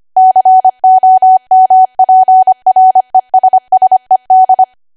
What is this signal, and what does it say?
Morse code, heard as a keyed tone, about 25 wpm, 750 Hz. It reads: COMPRESSED
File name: Archive / 25 WPM / 25WPM